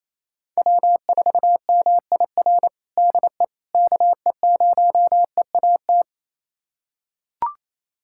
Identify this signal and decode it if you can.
Morse code, heard as a keyed tone, about 28 wpm, 700 Hz. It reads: W4MIR DE KE0EAT
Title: USA Callsign DE Callsign 28wpm